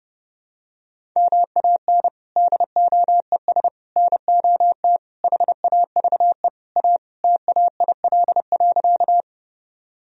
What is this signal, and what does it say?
Morse code, heard as a keyed tone, about 30 wpm, 700 Hz. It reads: MAN DOES NOT HAVE A TAIL.
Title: Sentences from Top 1000 Words - Review Part 3 - 30wpm